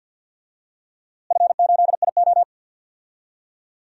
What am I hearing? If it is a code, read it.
Morse code, heard as a keyed tone, about 46 wpm, 700 Hz. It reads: F8IO
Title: DXCC-100 Call Signs - 4 Characters - Repeated 3x - 50wpm